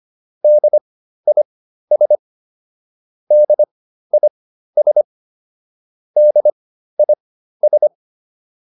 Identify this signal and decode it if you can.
Morse code, heard as a keyed tone, about 25 wpm, spaced as for 13 wpm, 600 Hz. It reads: DIS DIS DIS
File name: DIS7 13